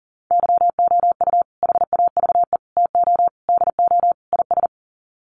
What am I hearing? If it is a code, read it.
Morse code, heard as a keyed tone, about 40 wpm, 700 Hz. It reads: YOU HAVE TO DO IS
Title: Sets of 5 Words 40wpm